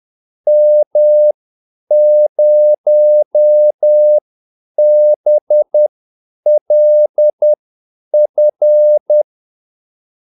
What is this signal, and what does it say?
Morse code, heard as a keyed tone, about 10 wpm, spaced as for 8 wpm, 600 Hz. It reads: M0BLF
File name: Callsigns-10-8wpm